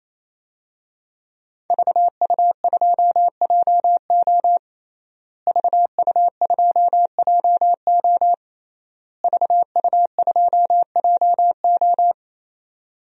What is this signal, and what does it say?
Morse code, heard as a keyed tone, about 28 wpm, 700 Hz. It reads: VU2JO VU2JO VU2JO
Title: DXCC-100 Call Signs - 5 Characters - Repeated 3x - 28wpm